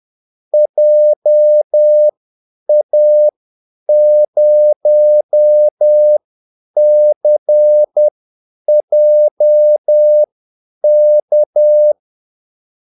Morse code, heard as a keyed tone, about 10 wpm, spaced as for 8 wpm, 600 Hz. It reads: JA0CJK
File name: Callsigns-10-8wpm